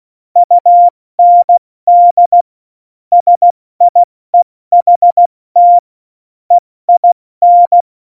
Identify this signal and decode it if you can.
Morse code, heard as a keyed tone, about 16 wpm, 700 Hz. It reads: UND SIEHT EIN
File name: Morsekurs / Witze / Witz 27. November 2018, 70 F80